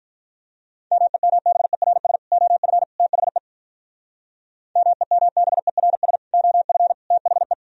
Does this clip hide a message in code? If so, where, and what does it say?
Yes: Morse code from 0.9 to 7.5 s, heard as a keyed tone, about 45 wpm, 700 Hz. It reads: MEMBERS OF THE MEMBERS OF THE